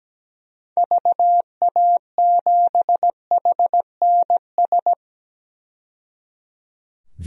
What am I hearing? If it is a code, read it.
Morse code, heard as a keyed tone, about 17 wpm, 700 Hz. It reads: VA7HNS
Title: DXCC-100 Call Signs - 6 Characters - 17wpm